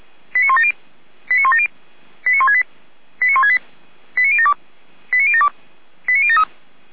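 Five-tone selective call (ZVEI), heard as quick runs of five tones, tones of about 70 ms. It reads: 89170 89170 89168 89168 89072 89072 89073
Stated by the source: ZVEI five-tone call received on SDR.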